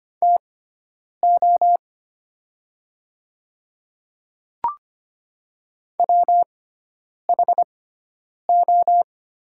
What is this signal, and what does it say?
Morse code, heard as a keyed tone, about 25 wpm, spaced as for 9 wpm, 700 Hz. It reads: TO WHO
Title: 6x Character Spacing - Top 100 Words - 25wpm